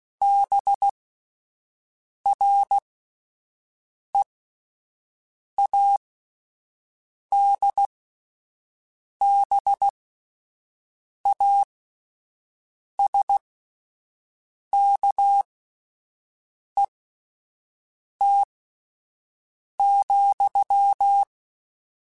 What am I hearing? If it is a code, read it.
Morse code, heard as a keyed tone, about 16 wpm, spaced as for 5 wpm, 800 Hz. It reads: BREADBASKET,